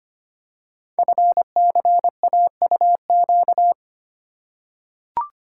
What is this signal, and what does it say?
Morse code, heard as a keyed tone, about 25 wpm, 700 Hz. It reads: FCAUQ
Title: Code Groups - 5 Letters - 25wpm